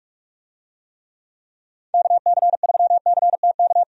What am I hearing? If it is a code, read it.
Morse code, heard as a keyed tone, about 47 wpm, 700 Hz. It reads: KC3CTK
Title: USA Call Signs - 6 Characters - Repeated 3x - 45wpm